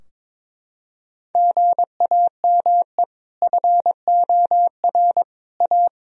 Morse code, heard as a keyed tone, about 22 wpm, 700 Hz. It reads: GAME FOR A
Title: Sentences from Top 1000 Words - Review Part 3 - 22wpm